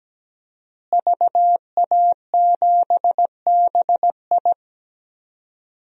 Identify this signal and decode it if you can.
Morse code, heard as a keyed tone, about 17 wpm, 700 Hz. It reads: VA7BI
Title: DXCC-100 Call Signs - 5 Characters - Repeated 2x - 17wpm